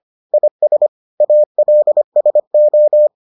Morse code, heard as a keyed tone, about 25 wpm, 600 Hz. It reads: IS ALSO